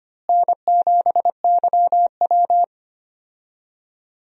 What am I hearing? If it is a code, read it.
Morse code, heard as a keyed tone, about 25 wpm, 700 Hz. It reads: N7YW
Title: USA Call Signs - 4 Characters - Repeated 3x - 25wpm